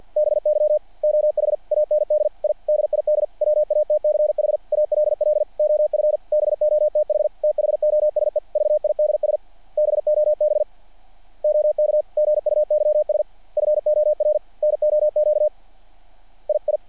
Morse code, heard as a keyed tone, about 48 wpm, 600 Hz. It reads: BY OH AND I DID WATCH ALL OF BOTH THOSE VIDS BOB OK GUYS FOR NOQ II
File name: archived / CRSnet